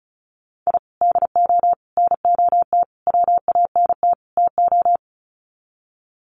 Morse code, heard as a keyed tone, about 34 wpm, 700 Hz. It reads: I DO NOT WANT TO
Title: Sets of 5 Words 35wpm